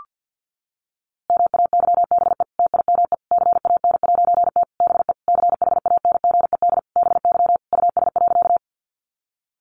Morse code, heard as a keyed tone, about 50 wpm, 700 Hz. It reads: MAYBE TIME CAN'T BE CHANGED BY US.
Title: A Sound Of Thunder 50wpm